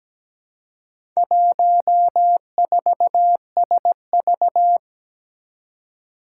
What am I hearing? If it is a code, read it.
Morse code, heard as a keyed tone, about 17 wpm, 700 Hz. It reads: 14SV
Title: Four Challenging Letter-Number Combinations 17wpm